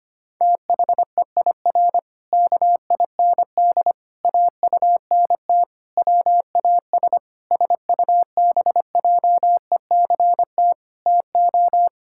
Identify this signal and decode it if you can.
Morse code, heard as a keyed tone, about 25 wpm, 700 Hz. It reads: THEIR KIND AUNT WAS SUBJECT TO
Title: Pangram Challenge v2 - 25wpm